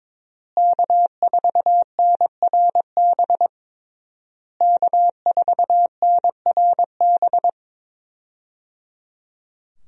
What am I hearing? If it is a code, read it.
Morse code, heard as a keyed tone, about 22 wpm, 700 Hz. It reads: K4NRB K4NRB
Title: USA Call Signs - 5 Characters - Repeated 3x - 22wpm